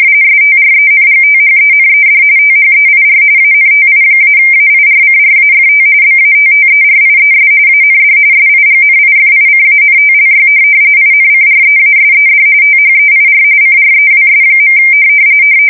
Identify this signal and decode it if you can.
RTTY, about 45.45 baud. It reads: STRENGTHENED THEIR ANTI-AIRCRAFT BATTERIES AND SIGNIFICANTLY INCREASED THEIR CREWS.
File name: fsk170 / USS-Massachusetts